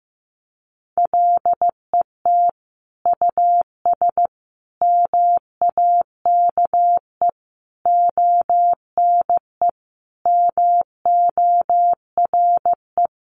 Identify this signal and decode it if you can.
Morse code, heard as a keyed tone, about 15 wpm, 700 Hz. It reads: LET US MAKE ONE MORE
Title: Sentences from Top 700 Words 15wpm